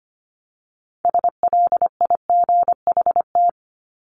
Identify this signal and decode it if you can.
Morse code, heard as a keyed tone, about 25 wpm, 700 Hz. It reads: SLIGHT
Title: LY Suffix - Top 100 Words with Base Words 25wpm